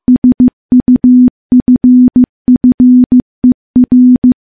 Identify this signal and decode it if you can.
Morse code, heard as a keyed tone, about 15 wpm, 250 Hz. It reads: SUFFER
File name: morse-training-15WPM